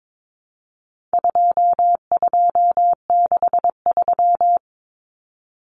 Morse code, heard as a keyed tone, about 22 wpm, 700 Hz. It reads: 2263
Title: Common Cut Numbers - 4-Digit 22wpm